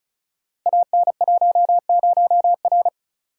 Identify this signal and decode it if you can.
Morse code, heard as a keyed tone, about 36 wpm, 700 Hz. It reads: AN10R
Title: Code Groups - 5 Letter:Numbers - 35wpm